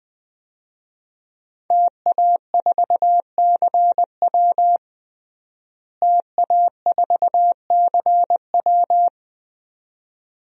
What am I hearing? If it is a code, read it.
Morse code, heard as a keyed tone, about 20 wpm, 700 Hz. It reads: TA4CW TA4CW
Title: DXCC-100 Call Signs - 5 Characters - Repeated 2x - 20wpm